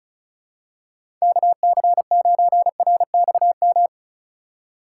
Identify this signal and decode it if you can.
Morse code, heard as a keyed tone, about 34 wpm, 700 Hz. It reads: KC9RXM
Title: USA Call Signs - 6 Characters - 6x Character Spacing - 35wpm